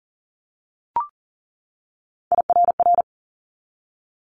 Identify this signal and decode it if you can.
Morse code, heard as a keyed tone, about 40 wpm, 700 Hz. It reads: IRR
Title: Multi-letter Phonemes 40wpm